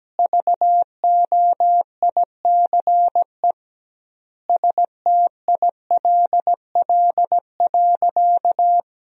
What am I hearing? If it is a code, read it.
Morse code, heard as a keyed tone, about 17 wpm, 700 Hz. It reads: VOICE STILL.
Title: 2x Word-Spacing - Sentences from Top 500 Words - 17wpm